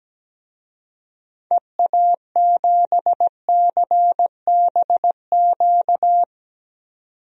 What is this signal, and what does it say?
Morse code, heard as a keyed tone, about 17 wpm, 700 Hz. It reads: EA7CBQ
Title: DXCC-100 Call Signs - 6 Characters - 17wpm